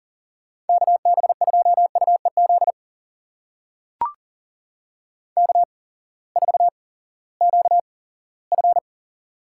Morse code, heard as a keyed tone, about 40 wpm, spaced as for 20 wpm, 700 Hz. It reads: KB2UEZ K4QF
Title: USA Call Signs - 8x Character Spacing - 40wpm